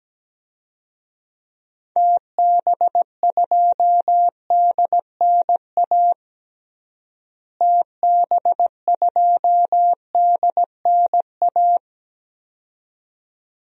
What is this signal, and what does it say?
Morse code, heard as a keyed tone, about 17 wpm, 700 Hz. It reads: TB2DNA TB2DNA
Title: DXCC-100 Call Signs - 6 Characters - Repeated 2x - 17wpm